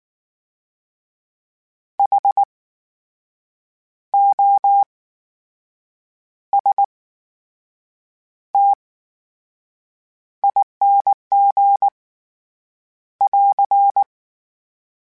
Morse code, heard as a keyed tone, about 19 wpm, spaced as for 6 wpm, 800 Hz. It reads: HOSTING <AR>
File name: Special INGs / Special INGs-mp3s / VerySlowToFaster Ing @18